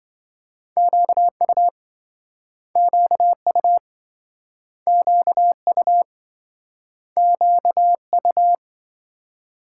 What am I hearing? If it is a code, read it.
Morse code, heard as a keyed tone, about 27 wpm, 700 Hz. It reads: QU QU QU QU